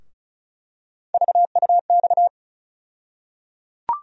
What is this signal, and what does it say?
Morse code, heard as a keyed tone, about 36 wpm, 700 Hz. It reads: VUX
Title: USA Call Sign Postfixes - 3 Letters 35wpm